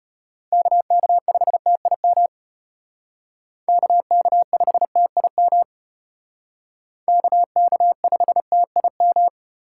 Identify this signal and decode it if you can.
Morse code, heard as a keyed tone, about 33 wpm, 700 Hz. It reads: KK5TIM KK5TIM KK5TIM